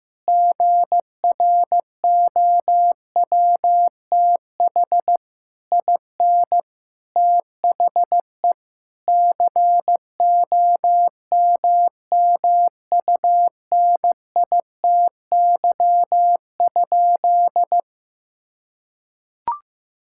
Morse code, heard as a keyed tone, about 16 wpm, 700 Hz. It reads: GROWTH IN THE COMMUNITY?
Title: Sentences from Top 1000 Words - Review Part 5 - 15wpm